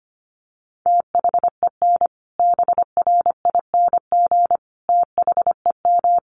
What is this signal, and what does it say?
Morse code, heard as a keyed tone, about 25 wpm, 700 Hz. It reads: THEN BRING THEM
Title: Sentences from Top 1500 Words 25wpm